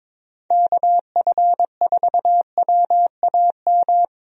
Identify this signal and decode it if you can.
Morse code, heard as a keyed tone, about 22 wpm, 700 Hz. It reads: KF4WAM
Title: USA Call Signs - 6 Characters - Repeated 2x - 22wpm